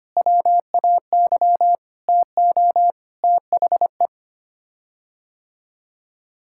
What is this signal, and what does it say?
Morse code, heard as a keyed tone, about 25 wpm, 700 Hz. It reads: WAY TO THE
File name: Sets of 9 Words - 25wpm